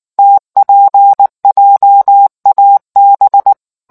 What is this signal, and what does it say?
Morse code, heard as a keyed tone, about 19 wpm, 800 Hz. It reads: TPJAB